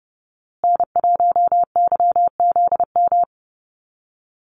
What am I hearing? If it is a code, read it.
Morse code, heard as a keyed tone, about 30 wpm, 700 Hz. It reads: N1YZM
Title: USA Call Signs Repeated 3x 30wpm